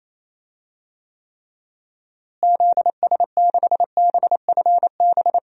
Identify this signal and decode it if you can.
Morse code, heard as a keyed tone, about 28 wpm, 700 Hz. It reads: ZS6BFB
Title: DXCC-100 Call Signs - 6 Characters - Repeated 2x - 28wpm